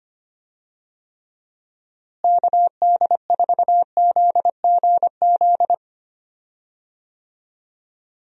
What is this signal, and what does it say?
Morse code, heard as a keyed tone, about 25 wpm, 700 Hz. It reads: KD4ZGZ
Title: USA Call Signs - 6 Characters - Repeated 3x - 25wpm